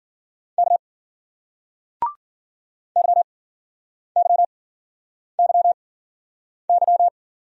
Morse code, heard as a keyed tone, about 55 wpm, 700 Hz. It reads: K Y Y Y Y